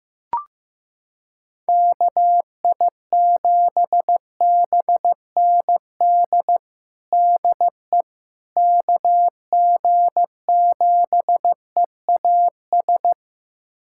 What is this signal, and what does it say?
Morse code, heard as a keyed tone, about 15 wpm, 700 Hz. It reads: KI7BND DE KG7EAS
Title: USA Callsign DE Callsign - 6 Characters - 15wpm